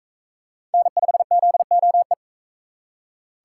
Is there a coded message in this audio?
Yes: Morse code from 0.7 to 2.1 s, heard as a keyed tone, about 41 wpm, 700 Hz. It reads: N5ZOE